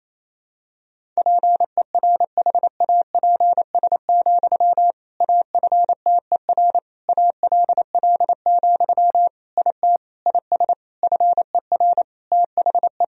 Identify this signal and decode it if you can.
Morse code, heard as a keyed tone, about 28 wpm, 700 Hz. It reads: PERHAPS, AFTER ALL, IT IS FER THE
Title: Sentences from Top 500 Words and Abbreviations 28wpm